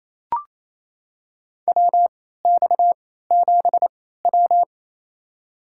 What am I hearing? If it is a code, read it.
Morse code, heard as a keyed tone, about 28 wpm, spaced as for 16 wpm, 700 Hz. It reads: WX7W
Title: USA Call Signs - 4 Characters - 3x Character Spacing - 28wpm